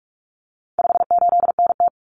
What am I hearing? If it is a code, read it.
Morse code, heard as a keyed tone, about 44 wpm, 700 Hz. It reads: 58NT
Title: Common Cut Numbers - 4-Digit 45wpm